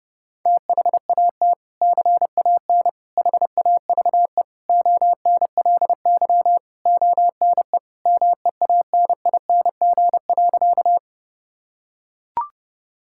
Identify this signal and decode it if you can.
Morse code, heard as a keyed tone, about 29 wpm, 700 Hz. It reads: THAT CAN HAVE ONLY ONE MEANING.
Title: Sentences from Top 1000 Words - Review Part 6 - 30wpm